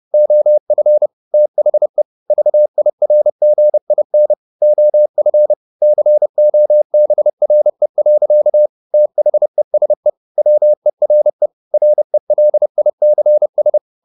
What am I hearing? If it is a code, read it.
Morse code, heard as a keyed tone, about 30 wpm, 600 Hz. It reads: OF THE VIRGIN OF COBRE. THESE WERE RELICS
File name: the old man and the sea / the old man and the sea 1 30wpm